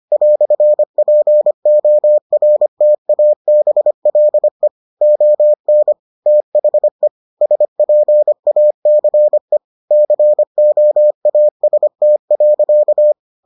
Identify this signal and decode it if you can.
Morse code, heard as a keyed tone, about 25 wpm, 600 Hz. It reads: "PORTABLE ON THE SPACE COAST.